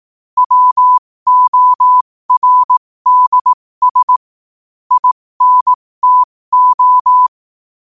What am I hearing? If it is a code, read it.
Morse code, heard as a keyed tone, about 18 wpm, spaced as for 15 wpm, 1000 Hz. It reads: WORDS INTO
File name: qotdfiles / 15wpm